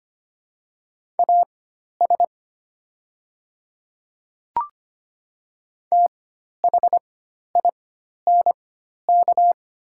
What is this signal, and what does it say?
Morse code, heard as a keyed tone, about 25 wpm, spaced as for 12 wpm, 700 Hz. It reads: AS THINK
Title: 4x Character Spacing - Top 300 Words - 25wpm